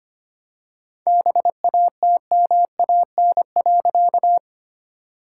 Morse code, heard as a keyed tone, about 25 wpm, 700 Hz. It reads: BATMAN.
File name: practice / 25wpm / All-Age Appropriate Jokes - 25wpm